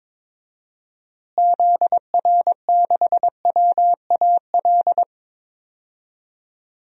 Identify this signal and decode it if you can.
Morse code, heard as a keyed tone, about 22 wpm, 700 Hz. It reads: ZR6WAL
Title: DXCC-100 Call Signs - 6 Characters - Repeated 3x - 22wpm